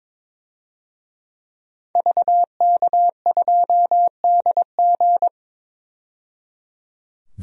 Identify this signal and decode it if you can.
Morse code, heard as a keyed tone, about 22 wpm, 700 Hz. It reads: VK2DG
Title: DXCC-100 Call Signs - 5 Characters - 22wpm